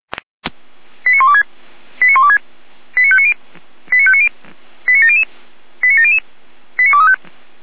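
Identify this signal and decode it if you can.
Five-tone selective call (ZVEI), heard as quick runs of five tones, tones of about 70 ms. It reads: 89126 89126 89590 89590 89700 89700 89235
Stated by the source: ZVEI five-tone call 89208 received by SDR, channel 455.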